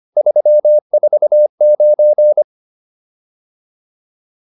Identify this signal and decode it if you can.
Morse code, heard as a keyed tone, about 25 wpm, 600 Hz. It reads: 349
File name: QSO408a 25